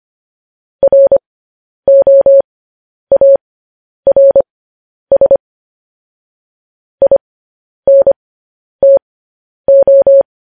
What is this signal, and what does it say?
Morse code, heard as a keyed tone, about 25 wpm, spaced as for 10 wpm, 550 Hz. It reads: ROARS INTO
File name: podcast / News headlines-1x-25x10